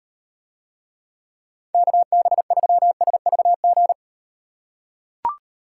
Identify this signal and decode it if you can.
Morse code, heard as a keyed tone, about 38 wpm, 700 Hz. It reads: KB3SVG